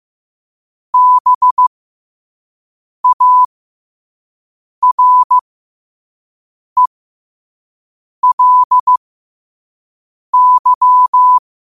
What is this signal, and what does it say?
Morse code, heard as a keyed tone, about 15 wpm, spaced as for 5 wpm, 1000 Hz. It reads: BARELY
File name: qotdfiles / 5wpm